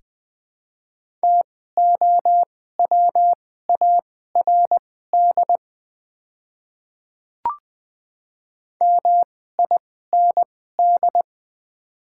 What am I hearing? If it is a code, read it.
Morse code, heard as a keyed tone, about 20 wpm, spaced as for 14 wpm, 700 Hz. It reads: TOWARD MIND
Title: 2x Character Spacing - Top 500 Words - 20wpm